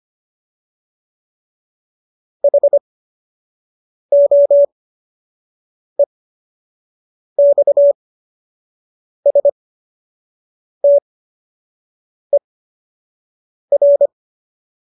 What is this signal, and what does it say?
Morse code, heard as a keyed tone, about 25 wpm, spaced as for 6 wpm, 570 Hz. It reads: HOEXSTER
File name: qso012 06